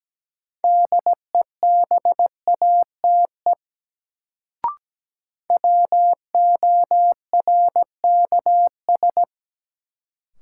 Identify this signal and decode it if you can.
Morse code, heard as a keyed tone, about 17 wpm, 700 Hz. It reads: DEBATE WORKS
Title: Sentences from Top 1100 Words - New Word Drill Alt - 17wpm